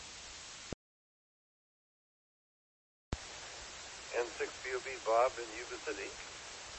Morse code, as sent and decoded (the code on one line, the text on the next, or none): none